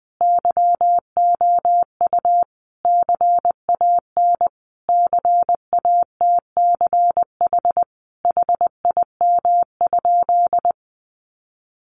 Morse code, heard as a keyed tone, about 20 wpm, 700 Hz. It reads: YOU CAN CATCH HIM?
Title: Sentences from Top 1500 Words 20wpm